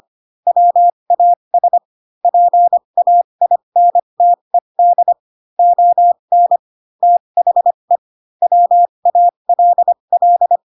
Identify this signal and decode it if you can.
Morse code, heard as a keyed tone, about 25 wpm, spaced as for 22 wpm, 700 Hz. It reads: WAS PAINTED ON THE WALL
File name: Pinocchio / Pinocchio ch3 18 25 ik1hge